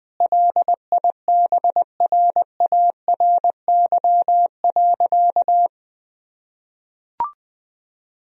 Morse code, heard as a keyed tone, about 21 wpm, 700 Hz. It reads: LIBRARY.
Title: Sentences from Top 1000 Words - Review Part 4 - 20wpm